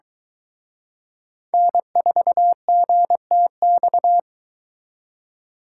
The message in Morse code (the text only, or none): N4GTX